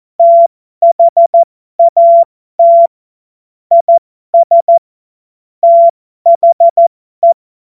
Morse code, heard as a keyed tone, about 14 wpm, spaced as for 12 wpm, 680 Hz. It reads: THAT IS THE
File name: Morsekurs / Witze